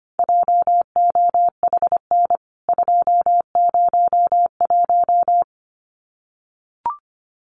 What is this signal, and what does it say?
Morse code, heard as a keyed tone, about 25 wpm, 700 Hz. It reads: JOHN 201
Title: CWT Exchange 25wpm